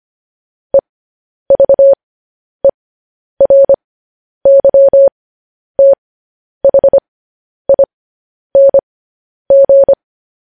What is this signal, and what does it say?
Morse code, heard as a keyed tone, about 25 wpm, spaced as for 10 wpm, 550 Hz. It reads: EVERYTHING
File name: podcast / News headlines-1x-25x10